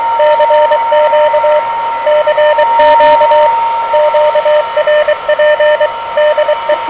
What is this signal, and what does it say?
Morse code, heard as a keyed tone, about 24 wpm, 590 Hz, with radio noise. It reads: CQ CQ QRP DE
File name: OK QRP 2023 / OK QRP 2023